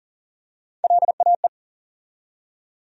Morse code, heard as a keyed tone, about 38 wpm, 700 Hz. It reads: LAE
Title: USA Call Sign Postfixes - 3 Letters 40wpm